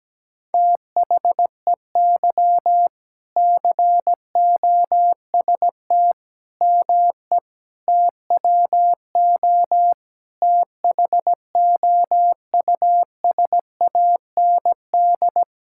Morse code, heard as a keyed tone, about 17 wpm, 700 Hz. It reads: THEY COST ME TWO THOUSAND
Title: Sentences from Top 1000 Words - Review Part 5 - 17wpm